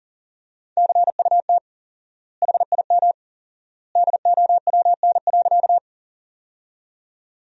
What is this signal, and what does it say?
Morse code, heard as a keyed tone, about 37 wpm, 700 Hz. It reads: CUT HIM DOWN.
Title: 4x Word-Spacing - Sentences from Top 300 Words - 40wpm